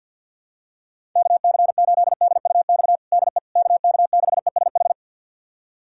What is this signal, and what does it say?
Morse code, heard as a keyed tone, about 49 wpm, 700 Hz. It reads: KC8DUX DE KK6ESH
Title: USA Callsign DE Callsign 50wpm